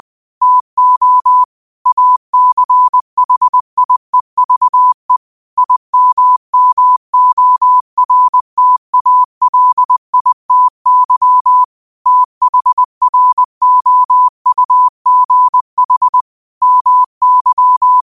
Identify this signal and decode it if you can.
Morse code, heard as a keyed tone, about 20 wpm, 1000 Hz. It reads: TO ACHIEVE IMMORTALITY THROUGH MY